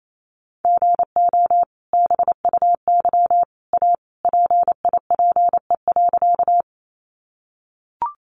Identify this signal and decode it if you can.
Morse code, heard as a keyed tone, about 28 wpm, 700 Hz. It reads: GO BUY A PIPE.
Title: Sentences from Top 1500 Words 28wpm